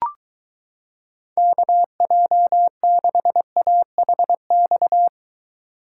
Morse code, heard as a keyed tone, about 23 wpm, 700 Hz. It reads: KJ6AHX